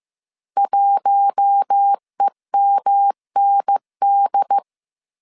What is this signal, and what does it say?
Morse code, heard as a keyed tone, about 15 wpm, 800 Hz. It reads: 1EMND